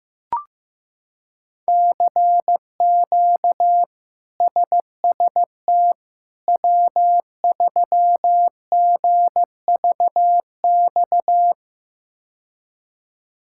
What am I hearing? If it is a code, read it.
Morse code, heard as a keyed tone, about 16 wpm, 700 Hz. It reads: CQ SST W3GVX
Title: K1USN SST QSO Practice - 15wpm